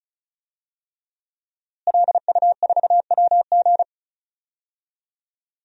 Morse code, heard as a keyed tone, about 34 wpm, 700 Hz. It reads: LU4WG
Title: DXCC-100 Call Signs - 5 Characters - Repeated 3x - 35wpm